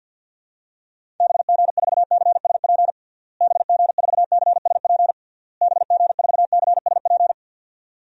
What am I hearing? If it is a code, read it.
Morse code, heard as a keyed tone, about 48 wpm, 700 Hz. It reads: BG4CSP BG4CSP BG4CSP
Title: DXCC-100 Call Signs - 6 Characters - Repeated 3x - 50wpm